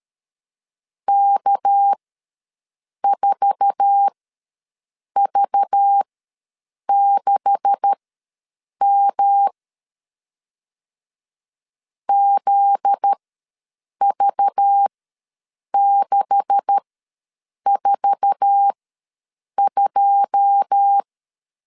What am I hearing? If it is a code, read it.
Morse code, heard as a keyed tone, about 13 wpm, spaced as for 7 wpm, 800 Hz. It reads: K4V6M ZV642
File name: LEZIONI / 9 286 ws 12wpm 3351